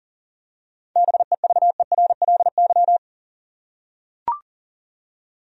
Morse code, heard as a keyed tone, about 40 wpm, 700 Hz. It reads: BEVERLY
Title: USA Common Names - 40wpm